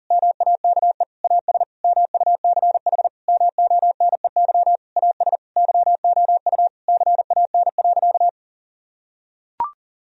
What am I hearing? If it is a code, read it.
Morse code, heard as a keyed tone, about 40 wpm, 700 Hz. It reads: MAKE AS MUCH MONEY AS YOU CAN.
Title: Sentences from Top 1000 Words - Review Part 3 - 40wpm